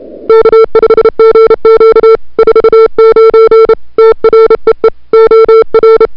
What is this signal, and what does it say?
Morse code, heard as a keyed tone, about 30 wpm, 440 Hz. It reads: K5GQ 49 TREE OR
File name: n6tr / Fab2007CWSprintHour1